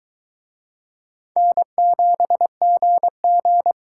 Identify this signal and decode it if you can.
Morse code, heard as a keyed tone, about 20 wpm, 700 Hz. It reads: N7GG